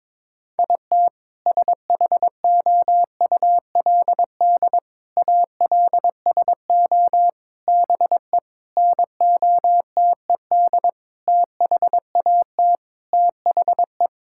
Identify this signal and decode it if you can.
Morse code, heard as a keyed tone, about 22 wpm, 700 Hz. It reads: IT SHOULD ALSO BE NOTED THAT THE
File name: Sentences with 7-Word Phrases - 22wpm